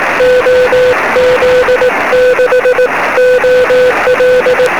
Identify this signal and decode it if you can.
Morse code, heard as a keyed tone, about 18 wpm, 500 Hz, with noise. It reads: OZ6OL
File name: sm7lcb / 17-oz6ol 23cm